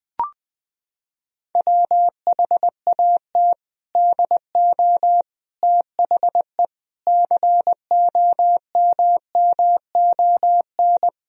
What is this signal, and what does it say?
Morse code, heard as a keyed tone, about 20 wpm, 700 Hz. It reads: WHAT DO THE COMMON
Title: Sentences from Top 1000 Words - Review Part 3 - 20wpm